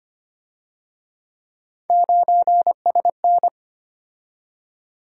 Morse code, heard as a keyed tone, about 30 wpm, 700 Hz. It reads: 9SN
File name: Speed-Racing - Three Letter-Number Combinations - 25wpm